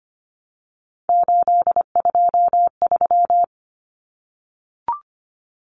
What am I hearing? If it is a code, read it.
Morse code, heard as a keyed tone, about 25 wpm, 700 Hz. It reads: 823
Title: Common Cut Numbers - 3-Digit 25wpm